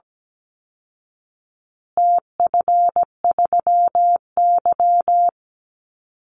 Morse code, heard as a keyed tone, about 17 wpm, 700 Hz. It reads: TF3Y
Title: Call Signs Repeated 3x - 17wpm